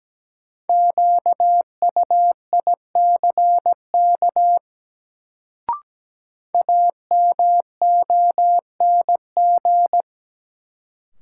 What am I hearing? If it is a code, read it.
Morse code, heard as a keyed tone, about 17 wpm, 700 Hz. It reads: QUICK AMONG
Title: Sentences from Top 500 Words - New Word Drill - 17wpm